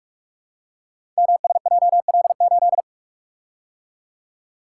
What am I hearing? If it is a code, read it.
Morse code, heard as a keyed tone, about 44 wpm, 700 Hz. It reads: MSJL8
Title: Code Groups - 5 Letter:Numbers - 45wpm